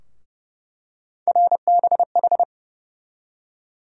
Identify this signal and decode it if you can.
Morse code, heard as a keyed tone, about 30 wpm, 700 Hz. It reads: RBH
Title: USA Call Sign Postfixes - 3 Letters 30wpm